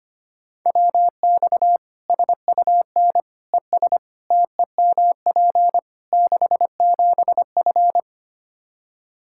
Morse code, heard as a keyed tone, about 25 wpm, 700 Hz. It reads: WX SUN ES TEMP 67F
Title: Practice QSO - Exchange 2 - Info - 25wpm